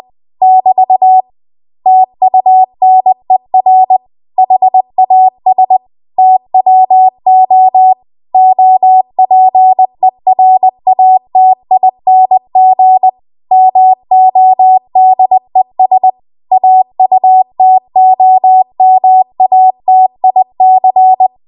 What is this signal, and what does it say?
Morse code, heard as a keyed tone, about 20 wpm, 750 Hz. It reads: <BT> TUNER HAS TWO OPERATING MODES AUTOMATIC
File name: Archive / 20 WPM / 20WPM